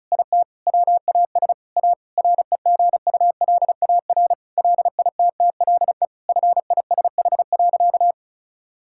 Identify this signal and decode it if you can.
Morse code, heard as a keyed tone, about 36 wpm, 700 Hz. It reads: IT WAS A REGULAR LITTLE FISH.
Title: Sentences from Top 1000 Words - Review Part 6 - 35wpm